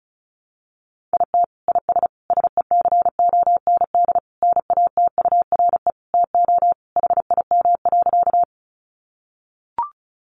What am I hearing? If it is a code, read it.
Morse code, heard as a keyed tone, about 34 wpm, 700 Hz. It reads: IT IS SECOND NATURE TO HIM.
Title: Sentences from Top 600 Words 35wpm